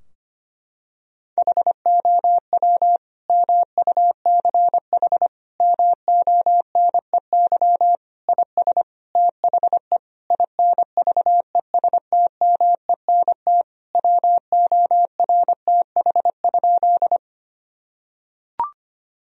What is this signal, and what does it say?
Morse code, heard as a keyed tone, about 25 wpm, 700 Hz. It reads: HOW MUCH MONEY IS THE INVESTMENT WORTH?
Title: Sentences from Top 1000 Words - Review Part 4 - 25wpm